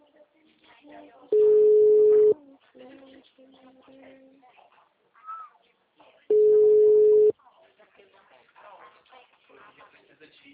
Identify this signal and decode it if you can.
Ringback tone (Europe, 425 Hz), with speech.
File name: out- -2015- - - .12